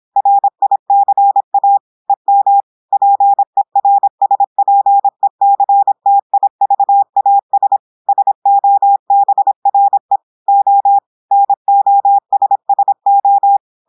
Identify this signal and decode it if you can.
Morse code, heard as a keyed tone, about 26 wpm, 800 Hz. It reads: RICA EM PERSPECTIVAS SOBRE O NOSSO
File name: audios / portugais / portugais texte en clair 1 vitesse 26 mots minutes